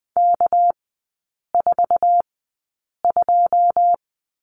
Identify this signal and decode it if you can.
Morse code, heard as a keyed tone, about 20 wpm, spaced as for 8 wpm, 700 Hz. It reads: K42
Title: 198 Lesson 32e - Letter k - Three Characters 20wpm